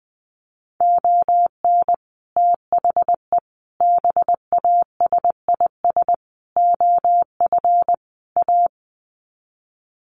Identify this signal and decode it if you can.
Morse code, heard as a keyed tone, about 20 wpm, 700 Hz. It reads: ON THE BASIS OF A
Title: Sets of 5 Words Encore 20wpm